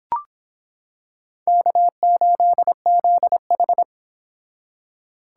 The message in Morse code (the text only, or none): K8ZH